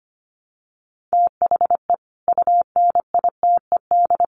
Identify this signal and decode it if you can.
Morse code, heard as a keyed tone, about 25 wpm, 700 Hz. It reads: THE UNITED
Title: Sets of 5 Words Encore 25wpm